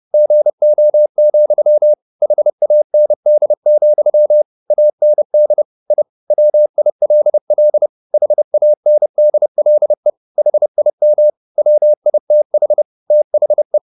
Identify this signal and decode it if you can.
Morse code, heard as a keyed tone, about 30 wpm, 600 Hz. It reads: GO, HAND, AND I WILL HANDLE HIM WITH THE
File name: the old man and the sea / the old man and the sea 1 30wpm